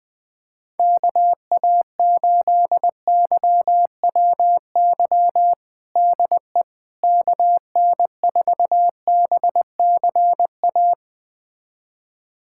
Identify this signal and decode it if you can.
Morse code, heard as a keyed tone, about 20 wpm, 700 Hz. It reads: KA8YWY DE KN4BCA
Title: USA Callsign DE Callsign - 6 Characters - 20wpm